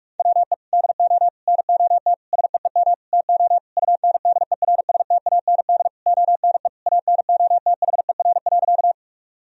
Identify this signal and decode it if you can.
Morse code, heard as a keyed tone, about 41 wpm, 700 Hz. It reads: WE DO NOT SEEM TO UNDERSTAND ONE ANOTHER.
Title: Sentences from Top 1000 Words - Review Part 1 - 45wpm